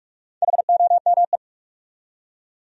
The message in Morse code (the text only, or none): HOME